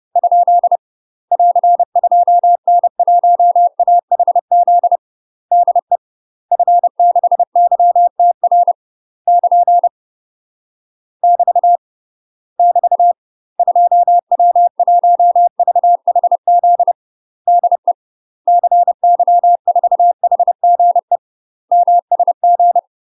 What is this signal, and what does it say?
Morse code, heard as a keyed tone, about 30 wpm, 700 Hz. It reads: ? <AR> 2N1AHZ DE F6YTR <KN> <BT> <BT> 2W1VHZ DE CY4HGE MSG
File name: UK MASTER QSOs TRAINING / 30WPM / UK MASTER QSOs 0028